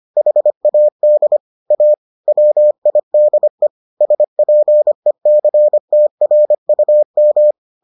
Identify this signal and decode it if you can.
Morse code, heard as a keyed tone, about 25 wpm, 600 Hz. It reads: HAD A WIDE SPECTRUM